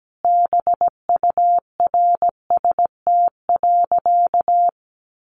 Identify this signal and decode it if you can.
Morse code, heard as a keyed tone, about 17 wpm, 700 Hz. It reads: BURST.
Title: A Sound Of Thunder 17wpm